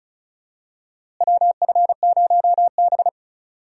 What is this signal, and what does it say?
Morse code, heard as a keyed tone, about 34 wpm, 700 Hz. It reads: WF0B
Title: USA Call Signs - 4 Characters - Repeated 3x - 35wpm